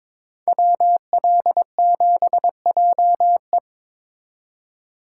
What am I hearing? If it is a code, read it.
Morse code, heard as a keyed tone, about 27 wpm, 700 Hz. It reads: WL7JE
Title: Speed Racing - USA 2x2 Call Signs 22wpm